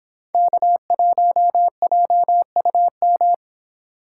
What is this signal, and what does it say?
Morse code, heard as a keyed tone, about 26 wpm, 700 Hz. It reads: K1JUM